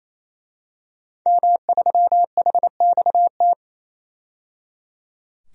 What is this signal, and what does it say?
Morse code, heard as a keyed tone, about 27 wpm, 700 Hz. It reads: M3HXT
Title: DXCC-100 Call Signs - 5 Characters - 28wpm